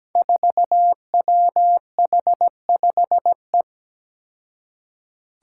Morse code, heard as a keyed tone, about 17 wpm, 700 Hz. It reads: 4WH5E
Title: Five Challenging Letter-Number Combinations 17wpm